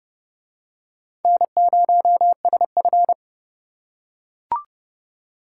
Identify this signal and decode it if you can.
Morse code, heard as a keyed tone, about 29 wpm, 700 Hz. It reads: N0SF